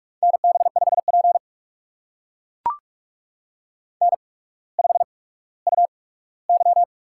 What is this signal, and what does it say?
Morse code, heard as a keyed tone, about 44 wpm, spaced as for 19 wpm, 700 Hz. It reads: NB5P N5UY
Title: USA Call Signs - 4 Characters - 8x Character Spacing - 45wpm